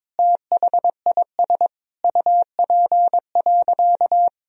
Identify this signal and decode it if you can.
Morse code, heard as a keyed tone, about 22 wpm, 700 Hz. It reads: THIS UP.
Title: Sentences from Top 1000 Words - Review Part 1 - 22wpm